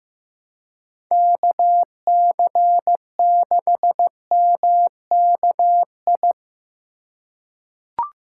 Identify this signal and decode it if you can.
Morse code, heard as a keyed tone, about 15 wpm, 700 Hz. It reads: KC6MKI
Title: USA Call Signs - 6 Characters - 4x Character Spacing - 15wpm